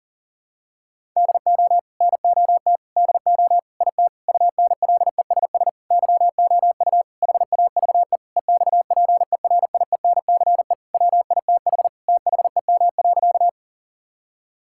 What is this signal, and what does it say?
Morse code, heard as a keyed tone, about 40 wpm, 700 Hz. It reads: DO NOT DO IT UNLESS YOU HAVE EXPERIENCE WITH THEM.
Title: Sentences from Top 1000 Words - Review Part 6 - 40wpm